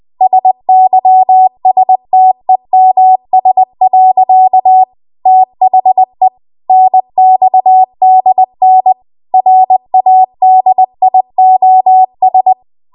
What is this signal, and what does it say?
Morse code, heard as a keyed tone, about 20 wpm, 750 Hz. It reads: SYSTEMS. THE NXDN RADIOS